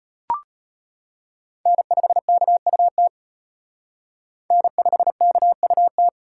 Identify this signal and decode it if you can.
Morse code, heard as a keyed tone, about 33 wpm, 700 Hz. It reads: N5KUT N5KUT